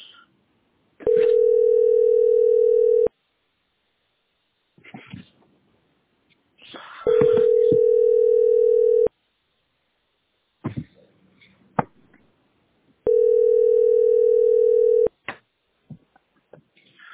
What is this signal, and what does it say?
Ringback tone (North America), 440 + 480 Hz.